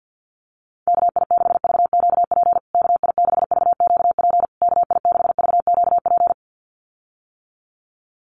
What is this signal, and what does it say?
Morse code, heard as a keyed tone, about 50 wpm, 700 Hz. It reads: KI6VQP KI6VQP KI6VQP
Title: USA Call Signs Repeated 3x 50wpm